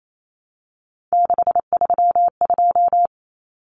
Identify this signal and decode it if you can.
Morse code, heard as a keyed tone, about 28 wpm, 700 Hz. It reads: 632
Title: Common Cut Numbers - 3-Digit 28wpm